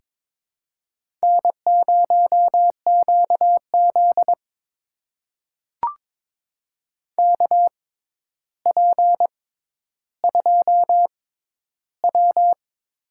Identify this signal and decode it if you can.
Morse code, heard as a keyed tone, about 22 wpm, spaced as for 11 wpm, 700 Hz. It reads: N0QZ KP2W
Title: USA Call Signs - 4 Characters - 6x Character Spacing - 22wpm